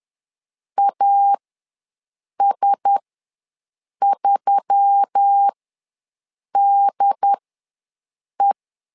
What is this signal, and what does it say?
Morse code, heard as a keyed tone, about 10 wpm, spaced as for 6 wpm, 800 Hz. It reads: AS3DE
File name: LEZIONI / 4 Q1S3 ws 10wpm 2437